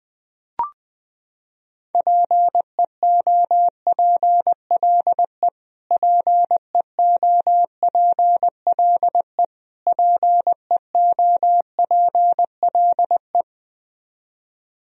Morse code, heard as a keyed tone, about 21 wpm, 700 Hz. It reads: PEOPLE PEOPLE PEOPLE
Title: Top 100 Words Repeated 20wpm